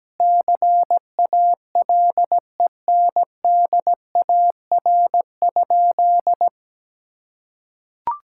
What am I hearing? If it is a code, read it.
Morse code, heard as a keyed tone, about 17 wpm, 700 Hz. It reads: CALENDAR?
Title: Sentences from Top 1600 Words 17wpm